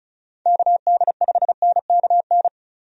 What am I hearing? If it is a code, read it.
Morse code, heard as a keyed tone, about 36 wpm, 700 Hz. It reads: KD5NKN
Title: USA Call Signs - 6 Characters - 35wpm